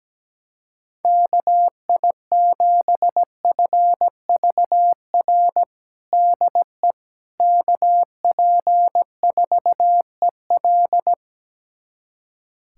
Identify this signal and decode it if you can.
Morse code, heard as a keyed tone, about 17 wpm, 700 Hz. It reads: KI7FVR DE KP4EL
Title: USA Callsign DE Callsign 17wpm